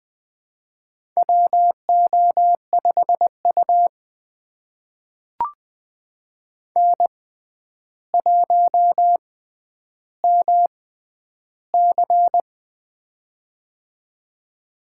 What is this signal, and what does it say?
Morse code, heard as a keyed tone, about 20 wpm, spaced as for 10 wpm, 700 Hz. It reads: WO5U N1MC
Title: USA Call Signs - 4 Characters - 6x Character Spacing - 20wpm